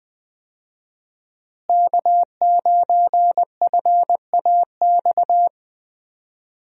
Morse code, heard as a keyed tone, about 20 wpm, 700 Hz. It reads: K9FAX
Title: USA Call Signs - 5 Characters - Repeated 3x - 20wpm